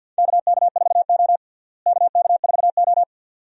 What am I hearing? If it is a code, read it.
Morse code, heard as a keyed tone, about 52 wpm, 700 Hz. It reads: KK4O KK4O
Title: USA Call Signs - 4 Characters - Repeated 2x - 50wpm